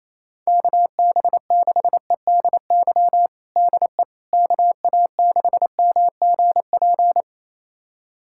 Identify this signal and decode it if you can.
Morse code, heard as a keyed tone, about 28 wpm, 700 Hz. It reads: KB6EDY DE KA6MGP
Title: USA Callsign DE Callsign - 6 Characters - 28wpm